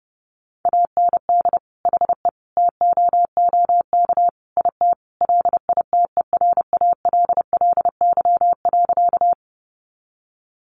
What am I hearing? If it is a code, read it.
Morse code, heard as a keyed tone, about 30 wpm, 700 Hz. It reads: AND HE TOOK IT LITERALLY.
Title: Sentences from Top 1400 Words 30wpm